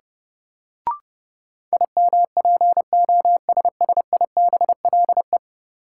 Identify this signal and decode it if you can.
Morse code, heard as a keyed tone, about 30 wpm, 700 Hz. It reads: IMPOSSIBLE
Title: Sentences from Top 1100 Words - New Word Drill - 30wpm